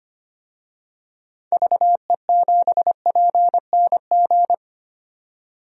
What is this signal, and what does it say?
Morse code, heard as a keyed tone, about 25 wpm, 700 Hz. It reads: VE7PNG
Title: DXCC-100 Call Signs - 6 Characters - 25wpm